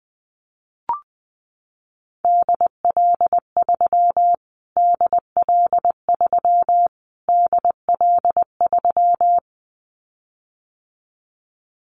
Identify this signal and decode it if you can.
Morse code, heard as a keyed tone, about 20 wpm, 700 Hz. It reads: DL3 DL3 DL3
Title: Callsign Prefixes for Top 25 DXCC - Repeated 3x - 20wpm